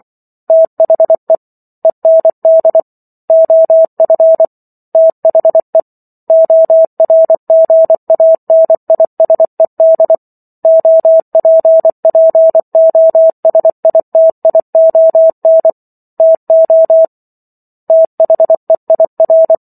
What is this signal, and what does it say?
Morse code, heard as a keyed tone, about 24 wpm, 650 Hz. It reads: THE END OF THE ORGANISED OPPOSITION TO THEIR